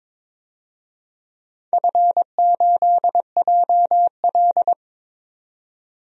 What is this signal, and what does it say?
Morse code, heard as a keyed tone, about 22 wpm, 690 Hz. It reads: F8JL
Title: DXCC-100 Call Signs - 4 Characters - 22wpm